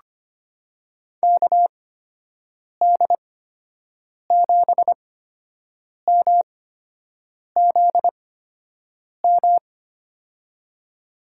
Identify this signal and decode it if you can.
Morse code, heard as a keyed tone, about 25 wpm, spaced as for 7 wpm, 700 Hz. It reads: KD7MZM
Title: USA Call Signs - 6 Characters - 8x Character Spacing - 25wpm